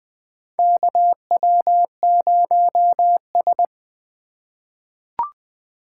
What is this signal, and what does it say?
Morse code, heard as a keyed tone, about 20 wpm, 700 Hz. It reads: KW0S
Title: USA Call Signs - 4 Characters - 4x Character Spacing - 20wpm